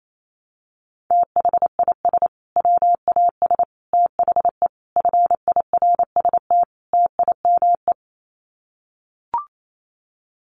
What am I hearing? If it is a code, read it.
Morse code, heard as a keyed tone, about 28 wpm, 700 Hz. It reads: THIS WAS THE FIRST TIME
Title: Sets of 5 Words Encore 28wpm